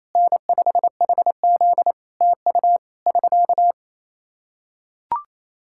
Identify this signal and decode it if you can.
Morse code, heard as a keyed tone, about 28 wpm, 700 Hz. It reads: N5HZ TU <SK>
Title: Practice QSO - Exchange 4 - Ending - 28wpm